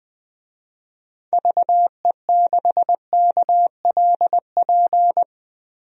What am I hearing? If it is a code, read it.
Morse code, heard as a keyed tone, about 19 wpm, 700 Hz. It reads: VE6KLP
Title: DXCC-100 Call Signs - 6 Characters - Repeated 2x - 20wpm